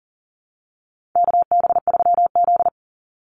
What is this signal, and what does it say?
Morse code, heard as a keyed tone, about 40 wpm, 700 Hz. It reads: KB3Z
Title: Call Signs Repeated 2x - 40wpm